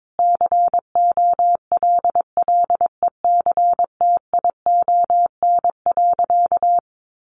Morse code, heard as a keyed tone, about 22 wpm, 700 Hz. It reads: COLLECTION.
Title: Sentences from Top 1200 Words 22wpm